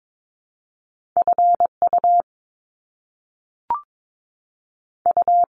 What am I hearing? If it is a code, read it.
Morse code, heard as a keyed tone, about 22 wpm, 700 Hz. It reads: F U U
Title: Top Two Letter Combinations in Words 22wpm